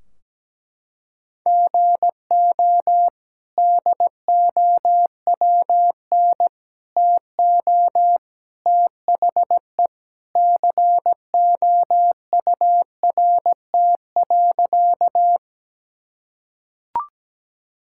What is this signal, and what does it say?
Morse code, heard as a keyed tone, about 17 wpm, 700 Hz. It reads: GO DOWN TO THE COURT.
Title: Sentences from Top 1600 Words 17wpm